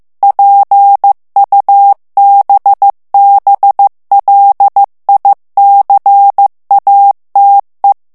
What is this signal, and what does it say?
Morse code, heard as a keyed tone, about 15 wpm, 800 Hz. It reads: PUBBLICATE
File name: morse mp3 / ITALIEN texte clair 2 vitesse 14 mots minute